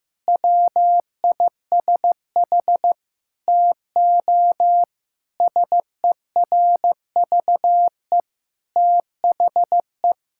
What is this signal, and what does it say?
Morse code, heard as a keyed tone, about 16 wpm, 700 Hz. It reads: WISH TO SERVE THE
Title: Sentences from Top 1600 Words 15wpm